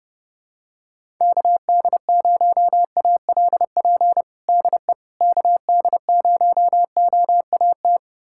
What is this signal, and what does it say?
Morse code, heard as a keyed tone, about 30 wpm, 700 Hz. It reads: KD0ALP DE KD0OAT
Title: USA Callsign DE Callsign 30wpm